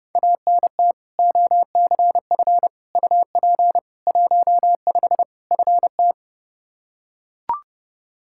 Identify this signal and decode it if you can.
Morse code, heard as a keyed tone, about 32 wpm, 700 Hz. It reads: ANT OCF UP 15 FT
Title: Practice QSO - Exchange 2 - Info - 30wpm